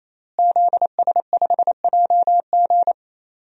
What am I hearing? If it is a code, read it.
Morse code, heard as a keyed tone, about 28 wpm, 700 Hz. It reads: ZS5JG
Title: DXCC-100 Call Signs - 5 Characters - Repeated 3x - 28wpm